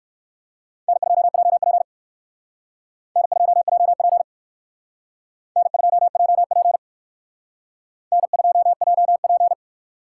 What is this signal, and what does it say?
Morse code, heard as a keyed tone, about 56 wpm, 700 Hz. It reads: N2JP N2JP N2JP N2JP